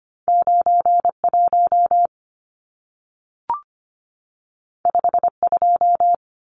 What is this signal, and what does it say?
Morse code, heard as a keyed tone, about 25 wpm, 700 Hz. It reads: 91 52
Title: Numbers - 2-Digit 25wpm